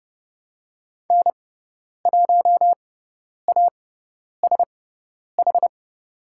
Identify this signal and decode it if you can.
Morse code, heard as a keyed tone, about 30 wpm, spaced as for 10 wpm, 700 Hz. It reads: N1ASH
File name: Lesson 16g - Letter c - Call Signs 30wpm